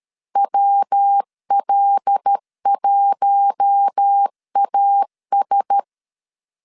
Morse code, heard as a keyed tone, about 13 wpm, 800 Hz. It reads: WL1AS
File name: LEZIONI / 6 PKX0 / 6 P 12wpm 2033